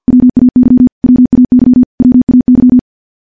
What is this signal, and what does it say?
Morse code, heard as a keyed tone, about 50 wpm, 250 Hz. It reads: WAY WAY WAY
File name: morse-training-50WPM